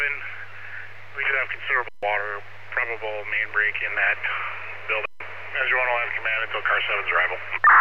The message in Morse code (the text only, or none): none